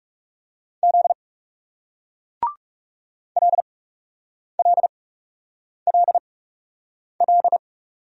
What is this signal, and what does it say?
Morse code, heard as a keyed tone, about 44 wpm, spaced as for 8 wpm, 700 Hz. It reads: Z LLLL